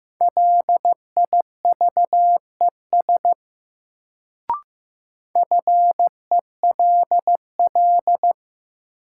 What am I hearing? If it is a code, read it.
Morse code, heard as a keyed tone, about 15 wpm, 700 Hz. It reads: LIVES FELL
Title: Sets of 5 Words Encore - New Word Drill - 15wpm